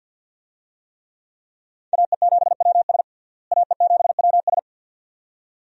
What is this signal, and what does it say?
Morse code, heard as a keyed tone, about 52 wpm, 700 Hz. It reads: AE7WS AE7WS
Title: USA Call Signs - 5 Characters - Repeated 2x - 50wpm